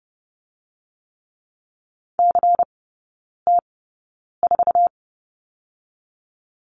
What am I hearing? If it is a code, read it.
Morse code, heard as a keyed tone, about 30 wpm, spaced as for 9 wpm, 700 Hz. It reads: CT4
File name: Lesson 16e - Letter c - Three Characters 30wpm